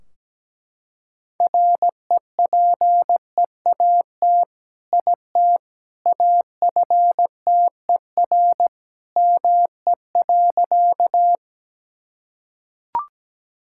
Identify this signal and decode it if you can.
Morse code, heard as a keyed tone, about 17 wpm, 700 Hz. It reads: REPEAT IT AFTER ME.
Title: Sentences from Top 1600 Words 17wpm